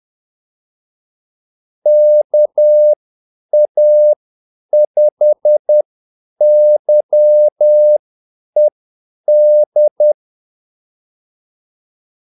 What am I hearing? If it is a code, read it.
Morse code, heard as a keyed tone, about 10 wpm, spaced as for 8 wpm, 600 Hz. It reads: KA5YED
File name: Callsigns-10-8wpm